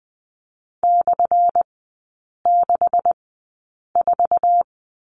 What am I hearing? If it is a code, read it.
Morse code, heard as a keyed tone, about 20 wpm, spaced as for 8 wpm, 700 Hz. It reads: /64
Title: 178 Lesson 29e - Slash - Three Characters 20wpm